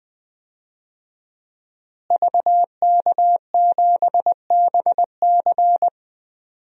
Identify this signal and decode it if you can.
Morse code, heard as a keyed tone, about 20 wpm, 700 Hz. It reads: VK7BC
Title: DXCC-100 Call Signs - 5 Characters - Repeated 3x - 20wpm